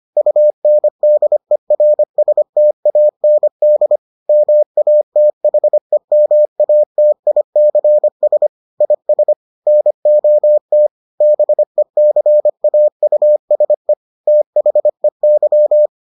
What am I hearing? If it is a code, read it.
Morse code, heard as a keyed tone, about 25 wpm, 600 Hz. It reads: UNDERSTAND MATHEMATICS IS NOT BECAUSE THEY